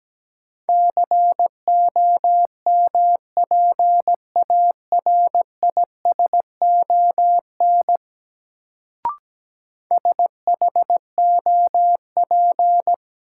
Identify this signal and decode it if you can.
Morse code, heard as a keyed tone, about 17 wpm, 700 Hz. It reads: COMPARISON SHOP
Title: Sentences from Top 1600 Words - New Word Drill Alt - 17wpm